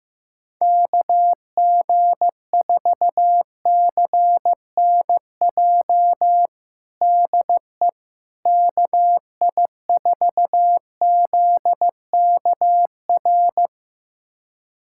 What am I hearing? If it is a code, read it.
Morse code, heard as a keyed tone, about 15 wpm, 700 Hz. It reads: KG4CNJ DE KI4ZKR
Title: USA Callsign DE Callsign - 6 Characters - 15wpm